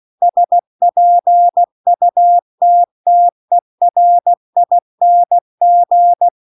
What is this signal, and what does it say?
Morse code, heard as a keyed tone, about 16 wpm, 700 Hz. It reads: SPUTTERING